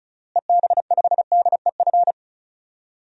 Morse code, heard as a keyed tone, about 36 wpm, 700 Hz. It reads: EB5DEF
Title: DXCC-100 Call Signs - 6 Characters - Repeated 3x - 35wpm